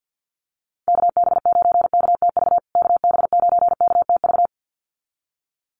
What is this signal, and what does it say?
Morse code, heard as a keyed tone, about 50 wpm, 700 Hz. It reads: KB9KTV KB9KTV
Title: USA Call Signs Repeated 2x 50wpm